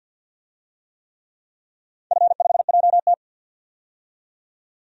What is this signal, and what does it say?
Morse code, heard as a keyed tone, about 50 wpm, 700 Hz. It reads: F5JT
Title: DXCC-100 Call Signs - 4 Characters - 50wpm